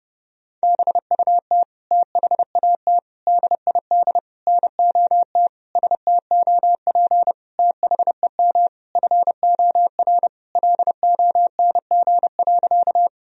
Morse code, heard as a keyed tone, about 30 wpm, 700 Hz. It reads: BUT THAT DID NOT STOP THEM FOR LONG.
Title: Sentences from Top 1000 Words - Review Part 1 - 30wpm